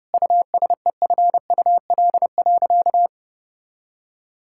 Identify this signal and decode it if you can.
Morse code, heard as a keyed tone, about 30 wpm, 700 Hz. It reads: USEFUL.
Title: Sentences from Top 1000 Words - Review Part 5 - 30wpm